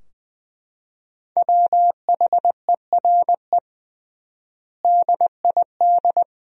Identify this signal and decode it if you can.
Morse code, heard as a keyed tone, about 20 wpm, 700 Hz. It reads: WHERE DID
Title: 3x Word-Spacing - Sentences from Top 200 Words - 20wpm